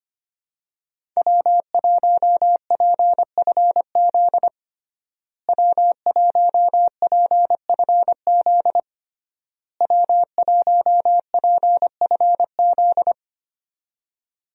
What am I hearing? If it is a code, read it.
Morse code, heard as a keyed tone, about 25 wpm, 700 Hz. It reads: W1PFZ W1PFZ W1PFZ
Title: USA Call Signs - 5 Characters - Repeated 3x - 25wpm